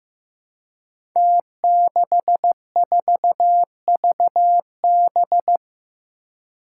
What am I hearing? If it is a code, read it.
Morse code, heard as a keyed tone, about 15 wpm, 700 Hz. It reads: T64VB
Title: Five Challenging Letter-Number Combinations 15wpm